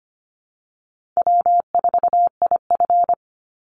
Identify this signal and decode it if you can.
Morse code, heard as a keyed tone, about 25 wpm, 700 Hz. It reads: W4IF
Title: Easy Call Signs Repeated 3x - 25wpm